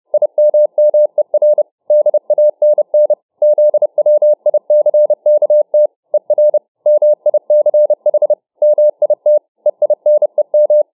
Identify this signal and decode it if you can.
Morse code, heard as a keyed tone, about 30 wpm, 600 Hz. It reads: IMMER DANN ZWICKT ER MICH MIT EINEM